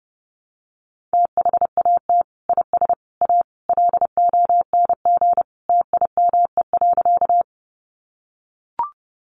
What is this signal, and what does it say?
Morse code, heard as a keyed tone, about 30 wpm, 700 Hz. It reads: THAT IS A LONG TIME.
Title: Sentences from Top 100 Words 30wpm